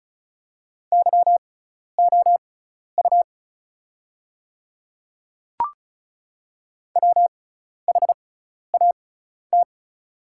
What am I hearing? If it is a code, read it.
Morse code, heard as a keyed tone, about 36 wpm, spaced as for 12 wpm, 700 Hz. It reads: YOU WHAT
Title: 6x Character Spacing - Top 100 Words - 35wpm